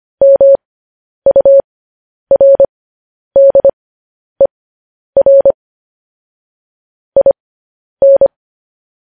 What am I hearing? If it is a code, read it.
Morse code, heard as a keyed tone, about 25 wpm, spaced as for 10 wpm, 550 Hz. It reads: MURDER IN
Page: news-headlines-in-morse-code-at-10-wpm